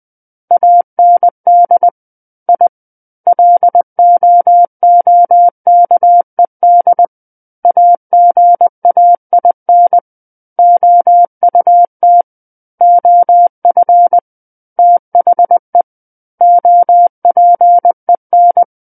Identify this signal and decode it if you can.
Morse code, heard as a keyed tone, about 20 wpm, 700 Hz. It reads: AND I LOOKED AGAIN OUT OF THE OPEN
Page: SKCC Learning Center - Morse Code Practice Files